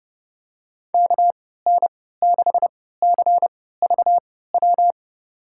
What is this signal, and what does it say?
Morse code, heard as a keyed tone, about 30 wpm, spaced as for 17 wpm, 700 Hz. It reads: KN6CVW
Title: USA Call Signs - 6 Characters - 3x Character Spacing - 30wpm